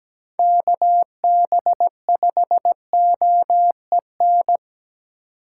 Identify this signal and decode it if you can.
Morse code, heard as a keyed tone, about 17 wpm, 700 Hz. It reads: KB5OEN
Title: USA Call Signs - 6 Characters - 4x Character Spacing - 17wpm